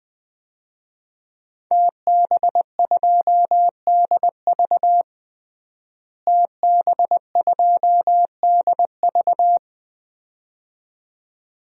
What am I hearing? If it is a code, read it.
Morse code, heard as a keyed tone, about 20 wpm, 700 Hz. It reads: TB2DV TB2DV
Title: DXCC-100 Call Signs - 5 Characters - Repeated 2x - 20wpm